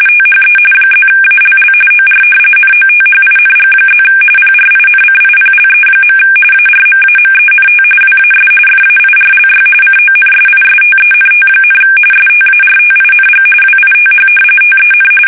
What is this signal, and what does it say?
RTTY, about 45.45 baud. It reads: THE +ALL OR NOTHING+ PRINCIPLE. ARMAMENT WAS FAIRLY CONSISTENT, STARTING WITH TEN
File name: fsk850 / battleships